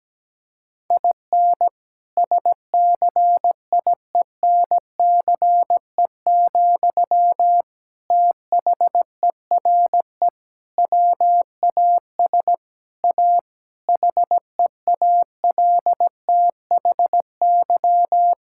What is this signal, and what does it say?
Morse code, heard as a keyed tone, about 17 wpm, 700 Hz. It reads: IN SCIENCE, THERE WAS A HEALTHY
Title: Sentences from Top 1000 Words - Review Part 5 - 17wpm